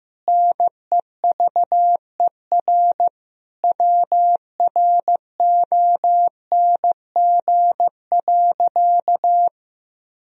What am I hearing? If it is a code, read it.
Morse code, heard as a keyed tone, about 16 wpm, 700 Hz. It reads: NEVER WRONG.
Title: Sentences from Top 1000 Words - Review Part 3 - 15wpm